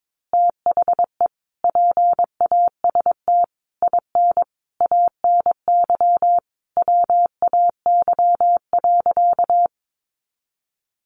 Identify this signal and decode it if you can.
Morse code, heard as a keyed tone, about 22 wpm, 700 Hz. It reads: THE PAST IN ANY WAY.
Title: A Sound Of Thunder 22wpm